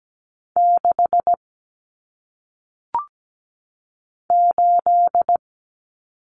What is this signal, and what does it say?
Morse code, heard as a keyed tone, about 17 wpm, 700 Hz. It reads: 6 8
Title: Numbers - 1-Digit 17wpm